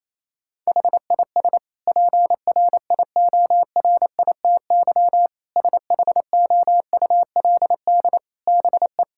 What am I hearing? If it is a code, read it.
Morse code, heard as a keyed tone, about 28 wpm, 700 Hz. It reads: HIS PRIORITY SHOULD BE
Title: Sentences from Top 1600 Words 28wpm